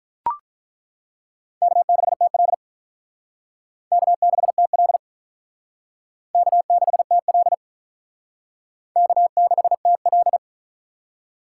Morse code, heard as a keyed tone, about 42 wpm, 700 Hz. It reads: K6TL K6TL K6TL K6TL
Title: Speed Racing - USA 1x2 Call Signs 35wpm